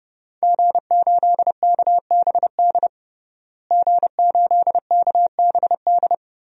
Morse code, heard as a keyed tone, about 30 wpm, 700 Hz. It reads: G8KBD G8KBD
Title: DXCC-100 Call Signs - 5 Characters - Repeated 2x - 30wpm